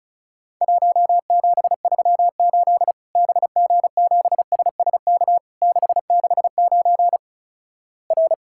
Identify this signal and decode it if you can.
Morse code, heard as a keyed tone, about 36 wpm, 700 Hz. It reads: 1738 BG7SSK 669 R
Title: Worked All Europe DX Contest - QTC Exchange - 35wpm